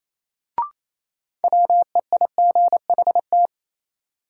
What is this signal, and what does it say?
Morse code, heard as a keyed tone, about 28 wpm, 700 Hz. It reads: WEIGHT
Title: Sentences from Top 800 Words - New Word Drill - 28wpm